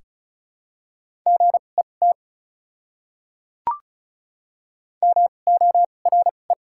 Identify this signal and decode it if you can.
Morse code, heard as a keyed tone, about 36 wpm, spaced as for 25 wpm, 700 Hz. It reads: GET MORE
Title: 2x Character Spacing - Top 200 Words - 35wpm